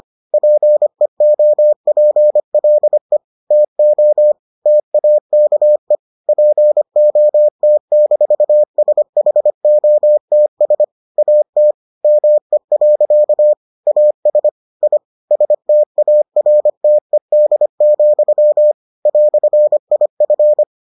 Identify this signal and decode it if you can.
Morse code, heard as a keyed tone, about 25 wpm, 600 Hz. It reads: PEOPLE TO TAKE POT-SHOTS AT ME. AS I STARTED, "IF